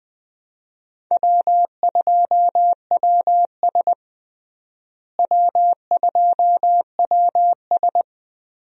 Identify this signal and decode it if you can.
Morse code, heard as a keyed tone, about 20 wpm, 700 Hz. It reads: W2WS W2WS
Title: USA Call Signs - 4 Characters - Repeated 3x - 20wpm